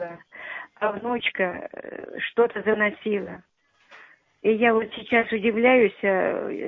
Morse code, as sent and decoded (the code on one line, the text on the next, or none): none